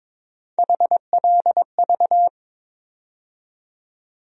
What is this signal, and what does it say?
Morse code, heard as a keyed tone, about 22 wpm, 700 Hz. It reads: HLV
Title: USA Call Sign Postfixes - 3 Letters 22wpm